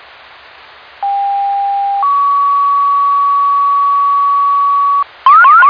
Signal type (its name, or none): two-tone page